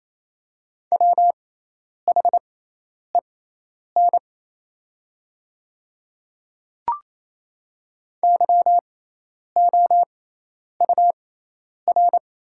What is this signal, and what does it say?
Morse code, heard as a keyed tone, about 27 wpm, spaced as for 10 wpm, 690 Hz. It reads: WHEN YOUR
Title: 6x Character Spacing - Top 100 Words - 28wpm